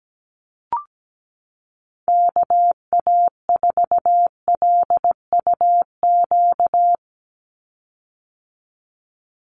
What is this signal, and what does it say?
Morse code, heard as a keyed tone, about 17 wpm, 700 Hz. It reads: KA4LUQ
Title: USA Call Signs 17wpm